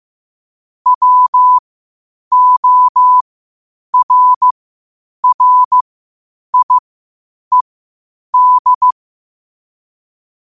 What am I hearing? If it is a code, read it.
Morse code, heard as a keyed tone, about 15 wpm, spaced as for 8 wpm, 1000 Hz. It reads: WORRIED
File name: qotdfiles / 8wpm